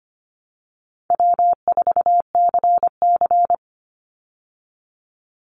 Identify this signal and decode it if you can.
Morse code, heard as a keyed tone, about 25 wpm, 700 Hz. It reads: W4CC
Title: Easy Call Signs Repeated 3x - 25wpm